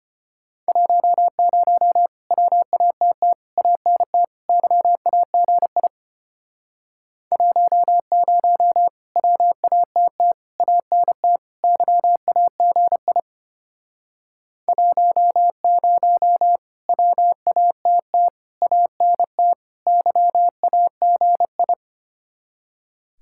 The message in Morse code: 10 WATT ANT YAGI 10 WATT ANT YAGI 10 WATT ANT YAGI

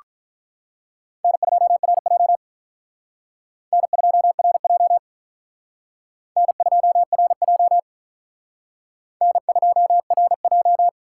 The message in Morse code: N2RJ N2RJ N2RJ N2RJ

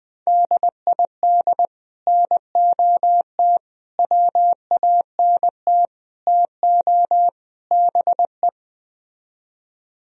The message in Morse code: DID NOT WANT TO BE

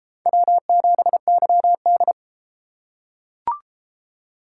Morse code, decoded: W7YD

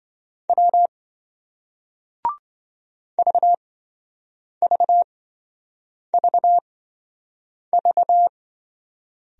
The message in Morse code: W VVVV

